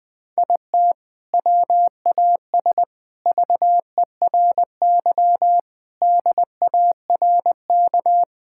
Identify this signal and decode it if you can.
Morse code, heard as a keyed tone, about 20 wpm, 700 Hz. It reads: IT WAS VERY DARK
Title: Sentences from Top 1000 Words - Review Part 6 - 20wpm